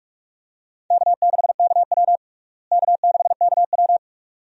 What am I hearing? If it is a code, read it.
Morse code, heard as a keyed tone, about 44 wpm, 700 Hz. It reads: K6KW K6KW
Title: USA Call Signs - 4 Characters - Repeated 2x - 45wpm